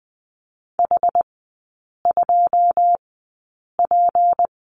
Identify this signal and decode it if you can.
Morse code, heard as a keyed tone, about 20 wpm, spaced as for 8 wpm, 700 Hz. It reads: H2P
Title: 152 Lesson 25e - Letter p - Three Characters 20wpm